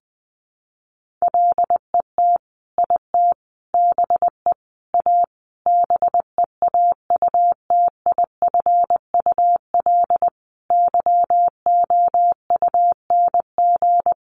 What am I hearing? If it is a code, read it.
Morse code, heard as a keyed tone, about 19 wpm, 700 Hz. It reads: LET IT BE A BEAUTIFUL YOUNG
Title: Sentences from Top 1100 Words 20wpm